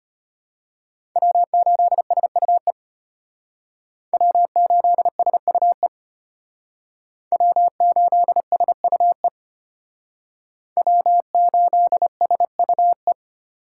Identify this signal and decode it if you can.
Morse code, heard as a keyed tone, about 32 wpm, 700 Hz. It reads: W8SUE W8SUE W8SUE W8SUE